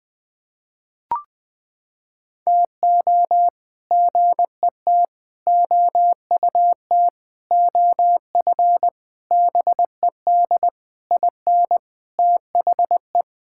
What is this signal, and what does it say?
Morse code, heard as a keyed tone, about 20 wpm, 700 Hz. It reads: TO GET OUT OF BED IN THE
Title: Sets of 8 Words - 20wpm